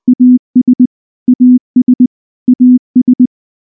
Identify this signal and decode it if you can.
Morse code, heard as a keyed tone, about 19 wpm, 250 Hz. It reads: AS AS AS
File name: morse / morse-training-20WPM-8